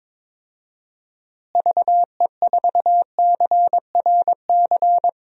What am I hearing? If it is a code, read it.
Morse code, heard as a keyed tone, about 22 wpm, 700 Hz. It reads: VE4CRC
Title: DXCC-100 Call Signs - 6 Characters - Repeated 2x - 22wpm